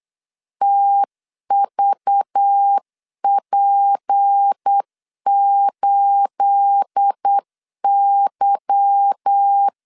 Morse code, heard as a keyed tone, about 8 wpm, 800 Hz. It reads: TVP8Y
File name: CW / DET / 1 9 1500C S A 8wpm 5637